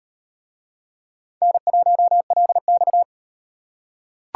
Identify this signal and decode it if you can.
Morse code, heard as a keyed tone, about 38 wpm, 700 Hz. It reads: N1LX